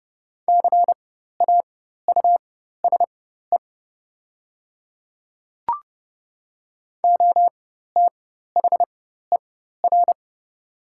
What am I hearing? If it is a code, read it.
Morse code, heard as a keyed tone, about 30 wpm, spaced as for 14 wpm, 700 Hz. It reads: CAUSE OTHER
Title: 4x Character Spacing - Top 200 Words - 30wpm